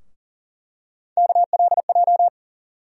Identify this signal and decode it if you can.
Morse code, heard as a keyed tone, about 40 wpm, 700 Hz. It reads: KLJ
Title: USA Call Sign Postfixes - 3 Letters 40wpm